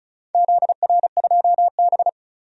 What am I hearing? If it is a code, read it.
Morse code, heard as a keyed tone, about 36 wpm, 700 Hz. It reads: ZR2B
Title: DXCC-100 Call Signs - 4 Characters - 35wpm